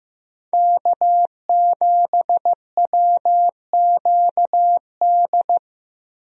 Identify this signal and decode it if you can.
Morse code, heard as a keyed tone, about 15 wpm, 700 Hz. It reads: K7WQD
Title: USA Call Signs - 5 Characters - 15wpm